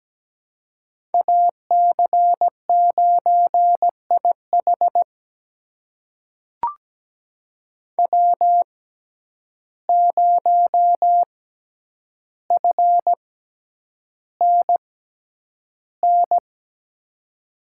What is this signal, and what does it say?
Morse code, heard as a keyed tone, about 17 wpm, spaced as for 9 wpm, 700 Hz. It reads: AC9IH W0FNN
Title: USA Call Signs - 5 Characters - 6x Character Spacing - 17wpm